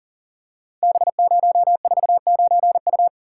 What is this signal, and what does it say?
Morse code, heard as a keyed tone, about 37 wpm, 700 Hz. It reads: B049U
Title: Code Groups - 5 Letter:Numbers - 40wpm